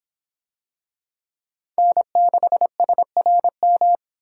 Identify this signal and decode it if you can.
Morse code, heard as a keyed tone, about 24 wpm, 700 Hz. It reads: N6SRM